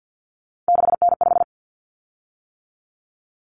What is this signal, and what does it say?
Morse code, heard as a keyed tone, about 50 wpm, 700 Hz. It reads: 6N5
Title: Common Cut Numbers - 3-Digit 50wpm